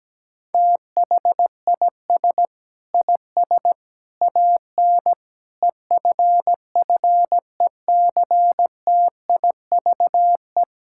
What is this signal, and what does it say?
Morse code, heard as a keyed tone, about 17 wpm, 700 Hz. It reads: THIS IS AN EFFECTIVE
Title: Sentences from Top 1000 Words - Review Part 6 - 17wpm